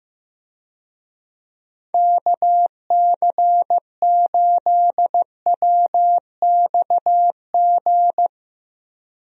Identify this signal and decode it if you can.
Morse code, heard as a keyed tone, about 15 wpm, 700 Hz. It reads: KC8WXG
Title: USA Call Signs - 6 Characters - Repeated 2x - 15wpm